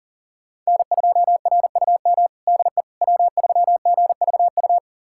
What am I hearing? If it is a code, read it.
Morse code, heard as a keyed tone, about 40 wpm, 700 Hz. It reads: N2RUM DE W3GVU
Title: USA Callsign DE Callsign - 5 Characters - 40wpm